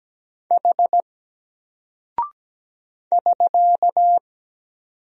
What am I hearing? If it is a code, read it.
Morse code, heard as a keyed tone, about 17 wpm, 700 Hz. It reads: H <SK>
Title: Single Character (Letter:Number:Punctuation:Prosign) Alt 17wpm